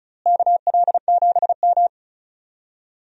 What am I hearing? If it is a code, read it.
Morse code, heard as a keyed tone, about 34 wpm, 700 Hz. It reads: KL7M
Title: DXCC-100 Call Signs - 4 Characters - Repeated 3x - 35wpm